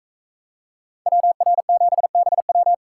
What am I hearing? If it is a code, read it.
Morse code, heard as a keyed tone, about 41 wpm, 700 Hz. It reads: WR7BW